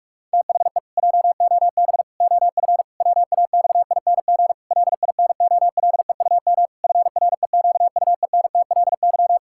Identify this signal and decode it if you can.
Morse code, heard as a keyed tone, about 43 wpm, 700 Hz. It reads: THE JOB OF WAXING LINOLEUM FREQUENTLY
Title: Pangram Challenge v2 - 45wpm